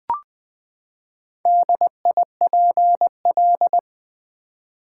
Morse code, heard as a keyed tone, about 20 wpm, 700 Hz. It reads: DIPL
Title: Build A Word - 4 Letters At A Time - 8-Letter Words - 20wpm